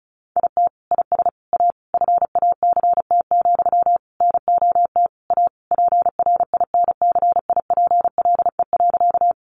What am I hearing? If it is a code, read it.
Morse code, heard as a keyed tone, about 34 wpm, 700 Hz. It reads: IT IS A FACT, NOT A PRINCIPLE.
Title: Sentences from Top 1400 Words 35wpm